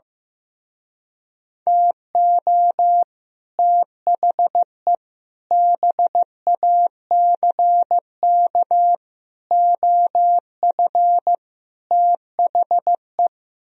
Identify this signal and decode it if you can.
Morse code, heard as a keyed tone, about 15 wpm, 700 Hz. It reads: TO THE BACK OF THE
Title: Speed-Racing - Sets of 5 Words 15wpm